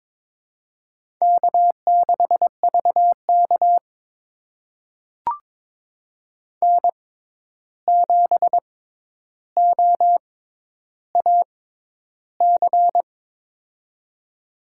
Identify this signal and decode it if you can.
Morse code, heard as a keyed tone, about 22 wpm, spaced as for 10 wpm, 700 Hz. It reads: K6VK N7OAC
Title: USA Call Signs - 6x Character Spacing - 22wpm